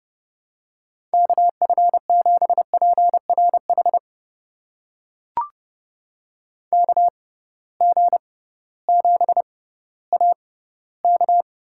Morse code, heard as a keyed tone, about 31 wpm, spaced as for 16 wpm, 700 Hz. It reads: KF7PRH KG7AK
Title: USA Call Signs - 6 Characters - 6x Character Spacing - 30wpm